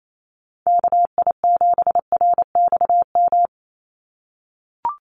Morse code, heard as a keyed tone, about 28 wpm, 700 Hz. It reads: KI7RXM
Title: USA Call Signs 28wpm